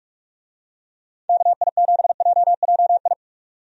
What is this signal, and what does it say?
Morse code, heard as a keyed tone, about 46 wpm, 700 Hz. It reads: KI7JJI
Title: USA Call Signs - 6 Characters - 45wpm